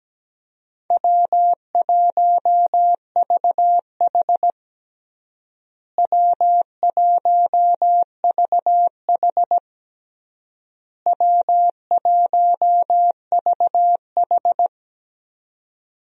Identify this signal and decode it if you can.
Morse code, heard as a keyed tone, about 17 wpm, 700 Hz. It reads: W1VH W1VH W1VH
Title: USA Call Signs - 4 Characters - Repeated 3x - 17wpm